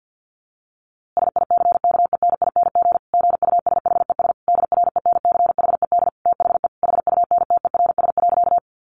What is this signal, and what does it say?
Morse code, heard as a keyed tone, about 49 wpm, 700 Hz. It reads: SICKENING GUSHES DRENCHED THE HUNTERS.
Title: A Sound Of Thunder 50wpm